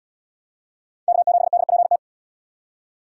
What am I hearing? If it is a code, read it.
Morse code, heard as a keyed tone, about 77 wpm, 700 Hz. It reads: K7DPT